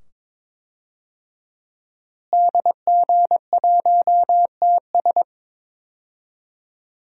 Morse code, heard as a keyed tone, about 22 wpm, 700 Hz. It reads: DG1TS
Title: DXCC-100 Call Signs - 5 Characters - Repeated 2x - 22wpm